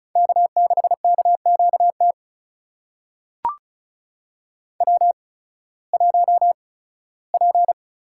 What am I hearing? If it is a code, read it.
Morse code, heard as a keyed tone, about 36 wpm, spaced as for 19 wpm, 700 Hz. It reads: K6KQT W1P
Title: USA Call Signs - 5 Characters - 8x Character Spacing - 35wpm